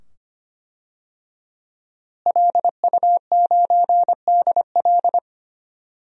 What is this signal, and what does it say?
Morse code, heard as a keyed tone, about 25 wpm, 700 Hz. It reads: LU9DL
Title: DXCC-100 Call Signs - 5 Characters - Repeated 2x - 25wpm